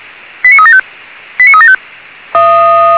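Five-tone selective call (ZVEI), heard as quick runs of five tones, tones of about 70 ms. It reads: 89375 89375
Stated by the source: ZVEI five-tone selective call received on SDR.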